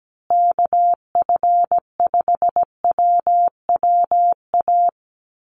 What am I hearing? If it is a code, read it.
Morse code, heard as a keyed tone, about 17 wpm, 700 Hz. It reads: KF5WWA
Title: USA Call Signs Repeated 3x 17wpm